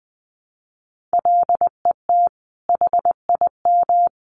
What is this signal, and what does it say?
Morse code, heard as a keyed tone, about 20 wpm, 700 Hz. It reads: LET HIM
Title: Sentences from Top 1200 Words 20wpm